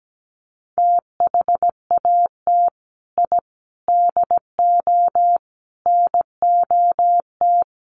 Morse code, heard as a keyed tone, about 17 wpm, 700 Hz. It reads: THAT I DO NOT
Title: Sets of 5 Words Encore 17wpm